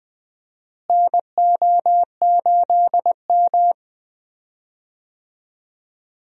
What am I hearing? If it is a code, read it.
Morse code, heard as a keyed tone, about 20 wpm, 700 Hz. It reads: NO8M
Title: USA Call Signs - 4 Characters - Repeated 2x - 20wpm